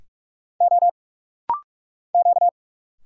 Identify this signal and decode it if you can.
Morse code, heard as a keyed tone, about 45 wpm, 700 Hz. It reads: O Q
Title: Single Letters - Rapid-Fire 45wpm